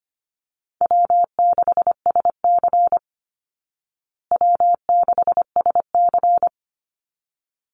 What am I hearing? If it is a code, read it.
Morse code, heard as a keyed tone, about 25 wpm, 700 Hz. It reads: W6SC W6SC
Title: Easy Call Signs Repeated 2x - 25wpm